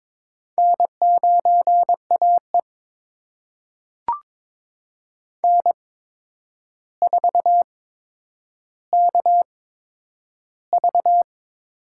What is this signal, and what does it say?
Morse code, heard as a keyed tone, about 22 wpm, spaced as for 9 wpm, 700 Hz. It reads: N9AE N4KV
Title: USA Call Signs - 4 Characters - 8x Character Spacing - 22wpm